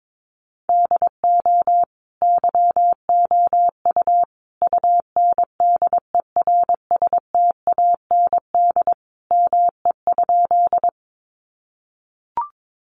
Morse code, heard as a keyed tone, about 22 wpm, 700 Hz. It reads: DO YOU UNDERSTAND ME?
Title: Sentences from Top 600 Words 22wpm